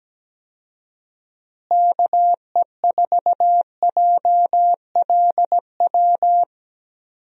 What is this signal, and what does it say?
Morse code, heard as a keyed tone, about 17 wpm, 700 Hz. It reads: KE4JLW
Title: USA Call Signs - 6 Characters - Repeated 2x - 17wpm